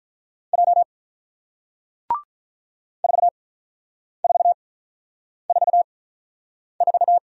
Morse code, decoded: J 4444

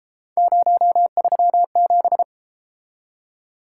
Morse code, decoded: 037

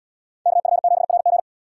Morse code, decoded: KF7RG

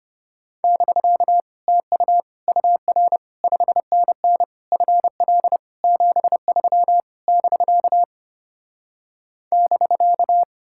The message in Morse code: <BK> TU UR 5NN FL 73 <BK> <BK>

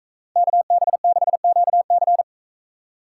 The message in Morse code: KB6QC

